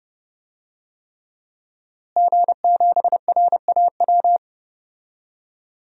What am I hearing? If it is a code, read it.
Morse code, heard as a keyed tone, about 30 wpm, 700 Hz. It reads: G7RAW
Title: DXCC-100 Call Signs - 5 Characters - Repeated 3x - 30wpm